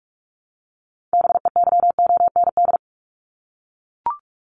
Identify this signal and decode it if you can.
Morse code, heard as a keyed tone, about 44 wpm, 700 Hz. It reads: BEYOND
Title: Top 751-1000 Words 45wpm